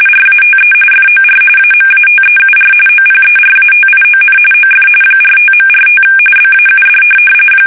RTTY, about 45.45 baud. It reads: RATE FOR AMATEUR OPERATION OF 45.45 BAUD
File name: fsk850 / teletype-RTTY